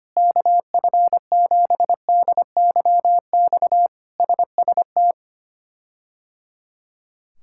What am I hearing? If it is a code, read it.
Morse code, heard as a keyed tone, about 25 wpm, 700 Hz. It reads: KF7DYX SST
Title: K1USN SST QSO Practice - 25wpm